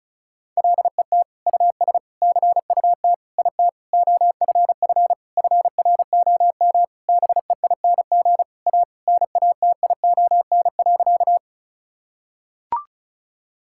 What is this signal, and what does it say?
Morse code, heard as a keyed tone, about 34 wpm, 700 Hz. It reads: LET US CUT IT OFF FROM BEING A NATION.
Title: Sentences from Top 1000 Words - Review Part 6 - 35wpm